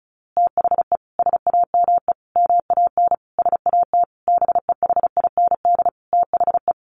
Morse code, heard as a keyed tone, about 34 wpm, 700 Hz. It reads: THE SAME MAN SAT BEHIND THE
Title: A Sound Of Thunder 35wpm